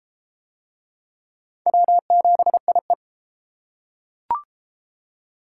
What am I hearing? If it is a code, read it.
Morse code, heard as a keyed tone, about 31 wpm, 700 Hz. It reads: W7IE